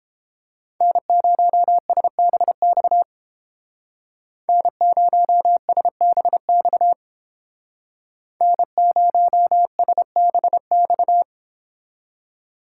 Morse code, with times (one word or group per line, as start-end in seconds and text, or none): 0.8-3.0 N0SBX
4.5-6.9 N0SBX
8.4-11.2 N0SBX